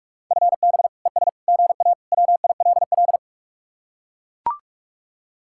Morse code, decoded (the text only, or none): FB ES GA WILL